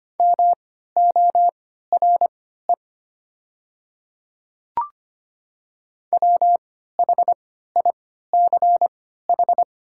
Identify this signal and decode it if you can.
Morse code, heard as a keyed tone, about 25 wpm, spaced as for 14 wpm, 700 Hz. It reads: MORE WHICH
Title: 3x Character Spacing - Top 200 Words - 25wpm